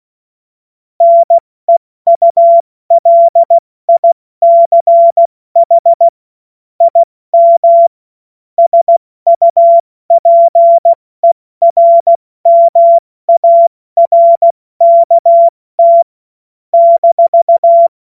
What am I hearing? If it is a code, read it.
Morse code, heard as a keyed tone, about 16 wpm, 680 Hz. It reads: NEULICH IM SUPERMARKT -
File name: Morsekurs / Witze / Witz 13. Juli 2021, 70 F80